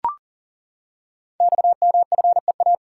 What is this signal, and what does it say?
Morse code, heard as a keyed tone, about 40 wpm, 690 Hz. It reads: XMFEA
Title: Code Groups - 5 Letter:Numbers - 40wpm